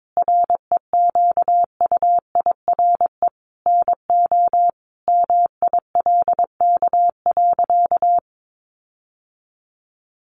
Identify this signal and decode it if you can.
Morse code, heard as a keyed tone, about 22 wpm, 700 Hz. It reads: REQUIRE NO MILK.
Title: Sentences from Top 1500 Words 22wpm